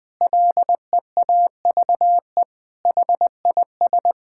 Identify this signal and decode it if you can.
Morse code, heard as a keyed tone, about 21 wpm, 700 Hz. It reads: LEAVE HIS
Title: Sentences from Top 1000 Words - Review Part 6 - 20wpm